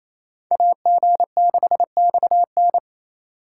Morse code, heard as a keyed tone, about 36 wpm, 700 Hz. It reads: AG6XN